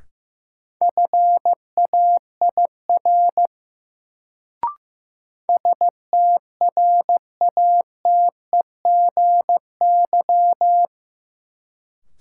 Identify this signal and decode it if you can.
Morse code, heard as a keyed tone, about 15 wpm, 700 Hz. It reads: FAIR STRATEGY
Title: Sentences from Top 1000 Words - Review Part 5 - New Word Drill - 15wpm